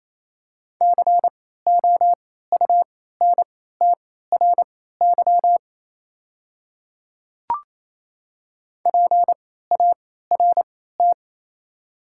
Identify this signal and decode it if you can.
Morse code, heard as a keyed tone, about 28 wpm, spaced as for 16 wpm, 700 Hz. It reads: COUNTRY PART
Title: 3x Character Spacing - Top 300 Words - 28wpm